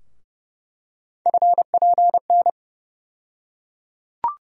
Morse code, heard as a keyed tone, about 30 wpm, 700 Hz. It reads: FPN
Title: USA Call Sign Postfixes - 3 Letters 30wpm